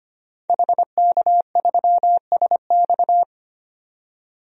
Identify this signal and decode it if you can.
Morse code, heard as a keyed tone, about 25 wpm, 700 Hz. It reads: HK3SX
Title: DXCC-100 Call Signs - 5 Characters - 25wpm